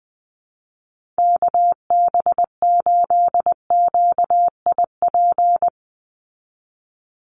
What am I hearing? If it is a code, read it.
Morse code, heard as a keyed tone, about 20 wpm, 700 Hz. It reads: KB8QIP
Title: USA Call Signs 20wpm